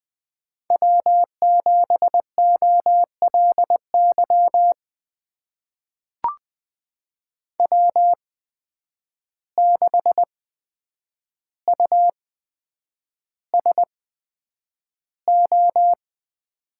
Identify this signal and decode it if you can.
Morse code, heard as a keyed tone, about 20 wpm, spaced as for 9 wpm, 700 Hz. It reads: W7OLY W6USO
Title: USA Call Signs - 5 Characters - 8x Character Spacing - 20wpm